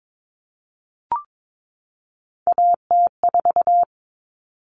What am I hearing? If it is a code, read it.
Morse code, heard as a keyed tone, about 22 wpm, 700 Hz. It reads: AT4
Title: Common Cut Numbers - 3-Digit 22wpm